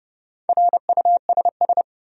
Morse code, heard as a keyed tone, about 32 wpm, 700 Hz. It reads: RUSS
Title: USA Common Names - 30wpm